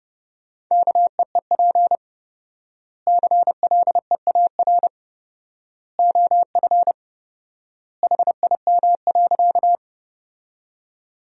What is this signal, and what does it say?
Morse code, heard as a keyed tone, about 32 wpm, 700 Hz. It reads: KEEP CLEAR OF HIM.
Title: 4x Word-Spacing - Sentences from Top 500 Words - 30wpm